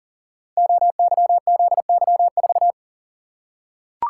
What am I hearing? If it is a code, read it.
Morse code, heard as a keyed tone, about 42 wpm, 690 Hz. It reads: OYZY4
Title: Code Groups - 5 Letter:Numbers - 40wpm